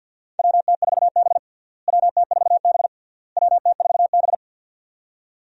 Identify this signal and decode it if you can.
Morse code, heard as a keyed tone, about 52 wpm, 700 Hz. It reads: WT4B WT4B WT4B
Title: USA Call Signs - 4 Characters - Repeated 3x - 50wpm